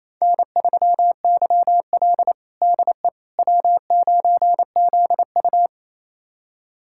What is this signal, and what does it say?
Morse code, heard as a keyed tone, about 28 wpm, 700 Hz. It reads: N3YL DE W9ZU
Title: USA Callsign DE Callsign - 4 Characters - 28wpm